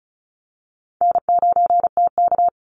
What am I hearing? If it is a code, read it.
Morse code, heard as a keyed tone, about 34 wpm, 700 Hz. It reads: N9TK
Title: Easy Call Signs Repeated 3x - 35wpm